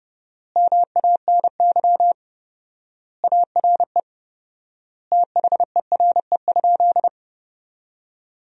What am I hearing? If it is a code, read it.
Morse code, heard as a keyed tone, about 30 wpm, 700 Hz. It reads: MANY ARE THERE?
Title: 4x Word-Spacing - Sentences from Top 100 Words - 30wpm